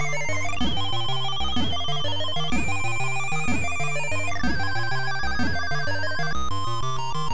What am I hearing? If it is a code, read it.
There is no code.